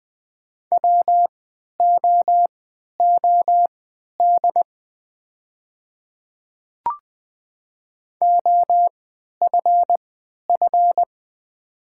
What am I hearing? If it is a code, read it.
Morse code, heard as a keyed tone, about 20 wpm, spaced as for 11 wpm, 700 Hz. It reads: WOOD OFF
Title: 3x Character Spacing - Top 300 Words - 20wpm